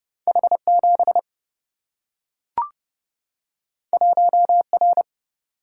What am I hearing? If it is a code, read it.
Morse code, heard as a keyed tone, about 30 wpm, 700 Hz. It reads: H 7 1 R
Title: Single-Letter Single-Number 30wpm